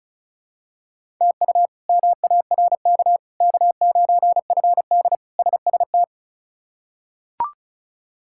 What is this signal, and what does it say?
Morse code, heard as a keyed tone, about 36 wpm, 700 Hz. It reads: TU MARK K9FD SST
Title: K1USN SST QSO Practice - 35wpm